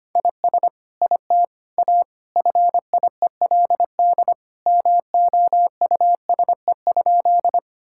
Morse code, heard as a keyed tone, about 25 wpm, 700 Hz. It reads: IS IT A FIELD MOUSE?
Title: Sentences from Top 1000 Words - Review Part 5 - 25wpm